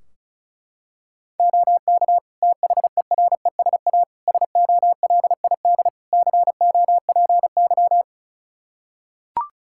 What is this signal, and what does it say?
Morse code, heard as a keyed tone, about 36 wpm, 700 Hz. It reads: OK THERESA SOLID COPY
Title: Practice QSO - Exchange 3 - More Info - 35wpm